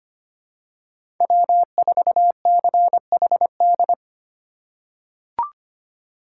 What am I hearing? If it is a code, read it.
Morse code, heard as a keyed tone, about 25 wpm, 700 Hz. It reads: W4CHD
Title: USA Call Signs - 5 Characters - 3x Character Spacing - 25wpm